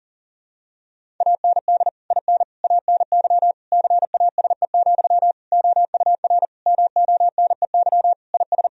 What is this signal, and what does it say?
Morse code, heard as a keyed tone, about 40 wpm, 700 Hz. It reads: AND IN ANY CASE, OUR MONEY IS
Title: Sentences from Top 1000 Words - Review Part 1 - 40wpm